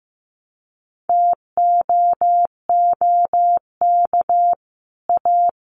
Morse code, heard as a keyed tone, about 15 wpm, 700 Hz. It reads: TOOK A
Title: Sets of 5 Words Encore 15wpm